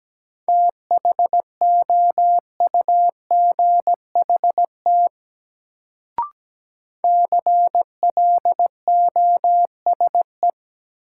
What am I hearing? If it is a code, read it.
Morse code, heard as a keyed tone, about 17 wpm, 700 Hz. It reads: THOUGHT CLOSE
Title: Sentences from Top 300 Words - New Word Drill - 17wpm